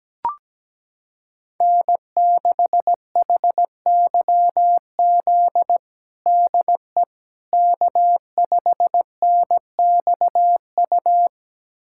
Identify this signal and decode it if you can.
Morse code, heard as a keyed tone, about 17 wpm, 700 Hz. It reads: N6HYZ DE K5NXU
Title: USA Callsign DE Callsign - 5 Characters - 17wpm